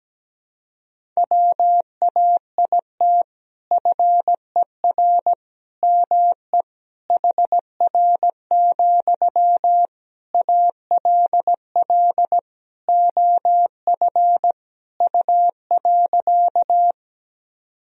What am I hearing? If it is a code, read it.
Morse code, heard as a keyed tone, about 17 wpm, 700 Hz. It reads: WAIT FER ME HR, ALL OF U.
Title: Sentences from Top 500 Words and Abbreviations 17wpm